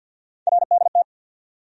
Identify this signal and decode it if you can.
Morse code, heard as a keyed tone, about 46 wpm, 700 Hz. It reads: RDT
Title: USA Call Sign Postfixes - 3 Letters 50wpm